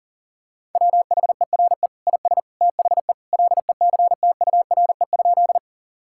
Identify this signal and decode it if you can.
Morse code, heard as a keyed tone, about 40 wpm, 700 Hz. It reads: WHERE IS THE LECTURE?
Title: Sentences from Top 1600 Words 40wpm